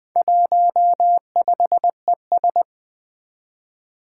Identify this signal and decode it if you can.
Morse code, heard as a keyed tone, about 21 wpm, 700 Hz. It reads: 15ES
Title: Four Challenging Letter-Number Combinations 20wpm